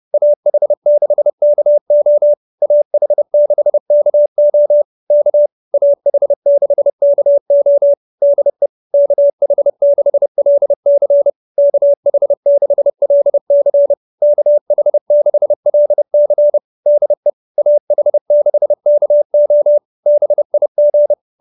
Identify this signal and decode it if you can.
Morse code, heard as a keyed tone, about 30 wpm, 590 Hz. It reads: AH6KO AH6KO K AH6KO DE KH6LC KH6LC KH6LC DE AH6KO BIG